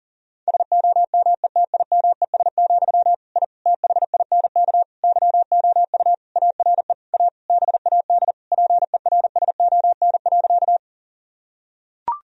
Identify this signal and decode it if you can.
Morse code, heard as a keyed tone, about 40 wpm, 700 Hz. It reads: SOMETIMES, I THINK YOU ARE A BAD PERSON.
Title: Sentences from Top 1000 Words - Review Part 3 - 40wpm